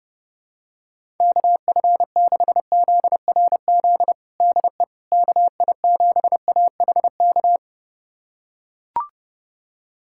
Morse code, heard as a keyed tone, about 32 wpm, 700 Hz. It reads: KF6ZRZ DE KI7AHK
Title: USA Callsign DE Callsign 30wpm